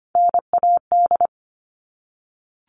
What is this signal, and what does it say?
Morse code, heard as a keyed tone, about 25 wpm, 700 Hz. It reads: NAD
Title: All Cut Numbers - 3-Digit 25wpm